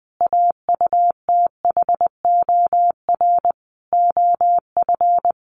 Morse code, heard as a keyed tone, about 20 wpm, 700 Hz. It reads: AUTHOR OF
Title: Sentences from Top 1500 Words 20wpm